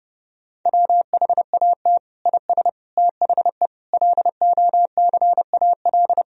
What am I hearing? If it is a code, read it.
Morse code, heard as a keyed tone, about 30 wpm, 700 Hz. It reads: WHAT IS THE LOCAL
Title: Sentences from Top 1000 Words - Review Part 3 - 30wpm